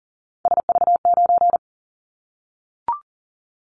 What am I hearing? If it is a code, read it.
Morse code, heard as a keyed tone, about 40 wpm, 700 Hz. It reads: SV9
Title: Callsign Prefixes for Top 100 DXCC - 40wpm